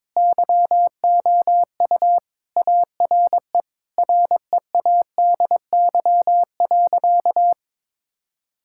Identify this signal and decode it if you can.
Morse code, heard as a keyed tone, about 22 wpm, 700 Hz. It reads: YOU ARE READY.
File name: practice / 22wpm / Sentences with 2-Word Phrases - 22wpm